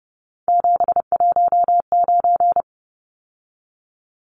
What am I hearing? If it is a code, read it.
Morse code, heard as a keyed tone, about 30 wpm, 700 Hz. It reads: 719
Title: Numbers - 3-Digit 30wpm